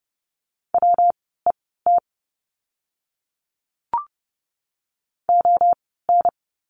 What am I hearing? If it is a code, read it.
Morse code, heard as a keyed tone, about 30 wpm, spaced as for 17 wpm, 700 Hz. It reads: WET ON
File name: course / 30:10wpm / 115 Lesson 19f - Letter w - Words 30wpm